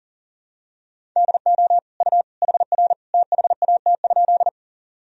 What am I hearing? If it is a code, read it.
Morse code, heard as a keyed tone, about 40 wpm, 700 Hz. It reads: DO U HR THAT?
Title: Sentences from Top 300 Words and Abbreviations 40wpm